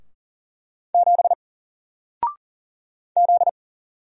Morse code, heard as a keyed tone, about 39 wpm, 700 Hz. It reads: 7 Z
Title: Single Character (Letter:Number:Punctuation:Prosign) Alt 40wpm